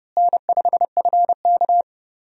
Morse code, heard as a keyed tone, about 30 wpm, 700 Hz. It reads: N5FK
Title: USA Call Signs - 4 Characters - Repeated 2x - 30wpm